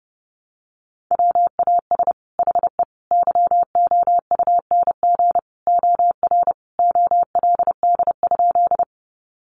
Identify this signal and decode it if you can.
Morse code, heard as a keyed tone, about 30 wpm, 700 Hz. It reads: WAS HE YOUNG OR OLD?
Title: Sentences from Top 400 Words 30wpm